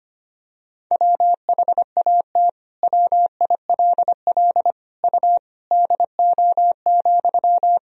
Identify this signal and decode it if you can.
Morse code, heard as a keyed tone, about 25 wpm, 700 Hz. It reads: WHAT WILL U DO,
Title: Sentences from Top 100 Words and Abbreviations 25wpm